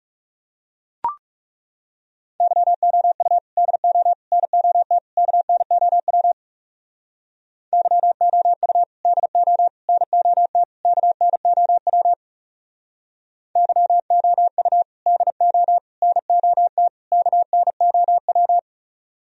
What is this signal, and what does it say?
Morse code, heard as a keyed tone, about 38 wpm, 700 Hz. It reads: YOU DO NOT KNOW YOU DO NOT KNOW YOU DO NOT KNOW